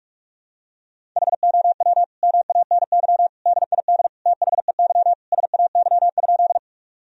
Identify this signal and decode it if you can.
Morse code, heard as a keyed tone, about 46 wpm, 700 Hz. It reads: HOW MANY DID THEY SAY?
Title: Sentences from Top 1000 Words - Review Part 1 - 45wpm